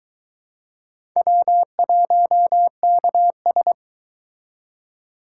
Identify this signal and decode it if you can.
Morse code, heard as a keyed tone, about 22 wpm, 700 Hz. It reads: W1KS